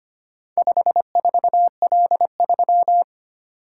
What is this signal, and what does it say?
Morse code, heard as a keyed tone, about 25 wpm, 700 Hz. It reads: 54L3
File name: practice / 25wpm / Four Letter-Number Combinations - 25wpm